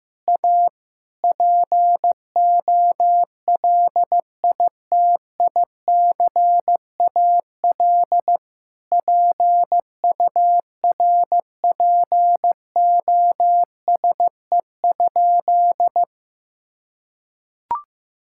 Morse code, heard as a keyed tone, about 15 wpm, 700 Hz. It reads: A POLITICAL PURPOSE?
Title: Sentences from Top 1000 Words - Review Part 5 - 15wpm